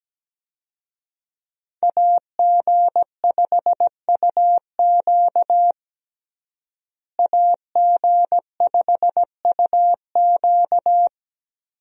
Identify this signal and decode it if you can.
Morse code, heard as a keyed tone, about 17 wpm, 700 Hz. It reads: AG5UQ AG5UQ
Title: USA Call Signs - 5 Characters - Repeated 3x - 17wpm